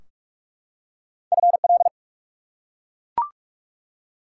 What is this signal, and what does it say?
Morse code, heard as a keyed tone, about 44 wpm, 700 Hz. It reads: FL
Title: US States and Canadian Provinces 45wpm